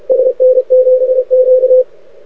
Morse code, heard as a keyed tone, about 32 wpm, 510 Hz, with radio noise. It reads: SN7Q
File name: rec contest / arrl cw 2018 / test 32